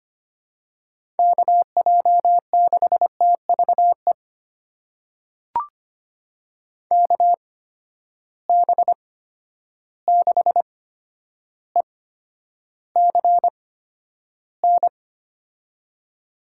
Morse code, heard as a keyed tone, about 25 wpm, spaced as for 11 wpm, 700 Hz. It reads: KJ6TVE KB6ECN
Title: USA Call Signs - 6 Characters - 8x Character Spacing - 25wpm